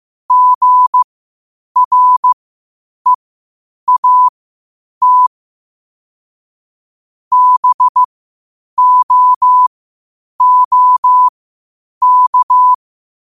Morse code, heard as a keyed tone, about 15 wpm, spaced as for 8 wpm, 1000 Hz. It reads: GREAT BOOK
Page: qotd-at-8-wpm